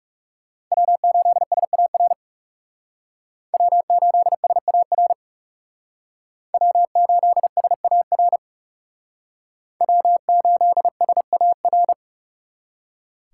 Morse code, heard as a keyed tone, about 36 wpm, 700 Hz. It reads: W8SAR W8SAR W8SAR W8SAR
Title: Speed Racing - USA 1x3 Call Signs 30wpm